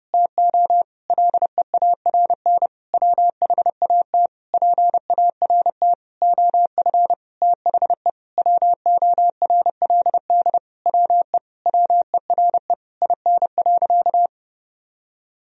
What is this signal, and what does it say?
Morse code, heard as a keyed tone, about 32 wpm, 700 Hz. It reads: TO LEARN WHAT PART OF THE WORLD WE WERE IN.
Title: Sentences from Top 1000 Words - Review Part 3 - 30wpm